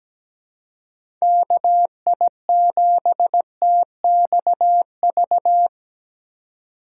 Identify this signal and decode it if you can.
Morse code, heard as a keyed tone, about 17 wpm, 700 Hz. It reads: KI7TXV
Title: USA Call Signs - 6 Characters - 2x Character Spacing - 17wpm